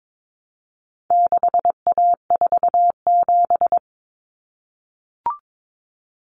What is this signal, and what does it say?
Morse code, heard as a keyed tone, about 22 wpm, 700 Hz. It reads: 6A47
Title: Common Cut Numbers - 4-Digit 22wpm